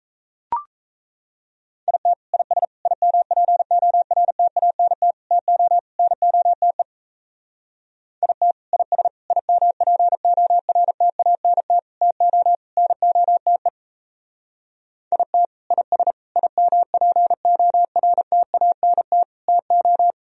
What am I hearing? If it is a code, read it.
Morse code, heard as a keyed tone, about 34 wpm, 700 Hz. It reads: IT IS IMPORTANT TO NOTE IT IS IMPORTANT TO NOTE IT IS IMPORTANT TO
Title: Speed-Racing - Sets of 5 Words 28wpm